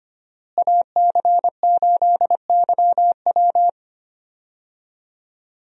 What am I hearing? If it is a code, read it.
Morse code, heard as a keyed tone, about 25 wpm, 700 Hz. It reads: AC8YW
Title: USA Call Signs - 5 Characters - Repeated 2x - 25wpm